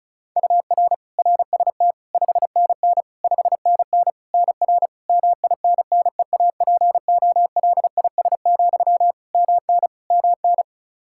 Morse code, heard as a keyed tone, about 36 wpm, 700 Hz. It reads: UR RST 5NN 5NN NR MINNEAPOLIS, MN MN
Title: Practice QSO - Exchange 1 - Beginning - 35wpm